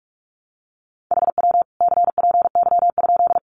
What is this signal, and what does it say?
Morse code, heard as a keyed tone, about 44 wpm, 700 Hz. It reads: HW CPY?
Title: Common QSO Phrases 45wpm